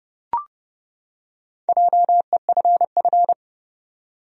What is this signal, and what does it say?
Morse code, heard as a keyed tone, about 30 wpm, 700 Hz. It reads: JEFF
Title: USA Common Names - Max 5 Letters - 30wpm